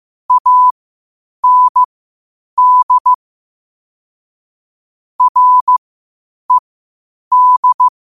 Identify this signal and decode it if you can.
Morse code, heard as a keyed tone, about 15 wpm, spaced as for 8 wpm, 1000 Hz. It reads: AND RED
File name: qotdfiles / 8wpm